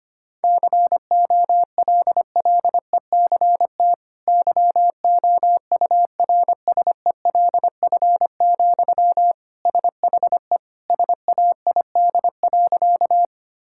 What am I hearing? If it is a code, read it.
Morse code, heard as a keyed tone, about 25 wpm, 700 Hz. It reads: COLLECT YOURSELF, SHE SAID.
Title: Sentences from Top 1600 Words 25wpm